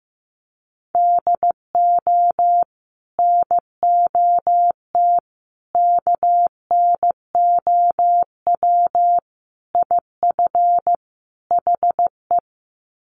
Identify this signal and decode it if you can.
Morse code, heard as a keyed tone, about 15 wpm, 700 Hz. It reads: DO NOT KNOW IF HE
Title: Sets of 5 Words Encore2 15wpm